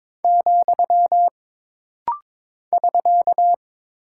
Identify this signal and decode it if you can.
Morse code, heard as a keyed tone, about 22 wpm, 700 Hz. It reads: , <SK>
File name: practice / 22wpm / Single Punctuation-Prosign - Rapid-Fire - 22wpm